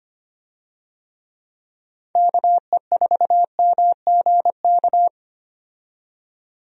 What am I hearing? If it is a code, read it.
Morse code, heard as a keyed tone, about 25 wpm, 700 Hz. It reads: KE4MGK
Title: USA Call Signs - 6 Characters - 25wpm